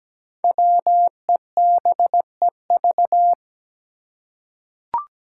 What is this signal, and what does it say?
Morse code, heard as a keyed tone, about 17 wpm, 700 Hz. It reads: WEBEV
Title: Five Challenging Letter-Number Combinations 17wpm